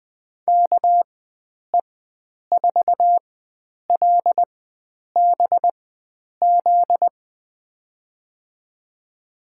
Morse code, heard as a keyed tone, about 21 wpm, spaced as for 9 wpm, 700 Hz. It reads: KE4LBZ
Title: USA Call Signs - 6 Characters - 4x Character Spacing - 20wpm